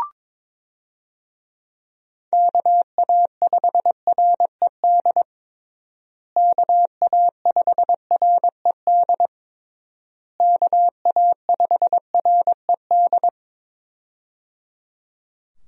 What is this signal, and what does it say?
Morse code, heard as a keyed tone, about 22 wpm, 700 Hz. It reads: KA5RED KA5RED KA5RED
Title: USA Call Signs - 6 Characters - Repeated 3x - 22wpm